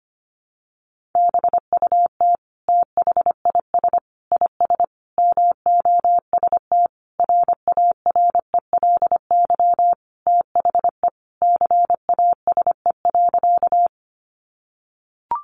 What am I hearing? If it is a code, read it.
Morse code, heard as a keyed tone, about 25 wpm, 700 Hz. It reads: BUT THIS IS MOST RARELY THE CASE.
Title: Sentences from Top 1400 Words 25wpm